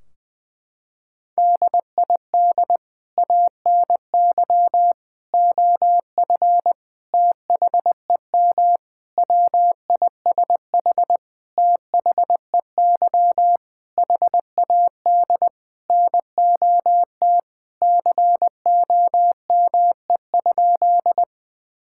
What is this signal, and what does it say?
Morse code, heard as a keyed tone, about 21 wpm, 700 Hz. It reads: DID ANY OF THEM WISH THEY HAD NOT COME?
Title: Sentences from Top 1600 Words 20wpm